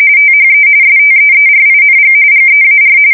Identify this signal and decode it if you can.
RTTY, about 45.45 baud. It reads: THE STEERING GEAR
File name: fsk170 / USS-Missouri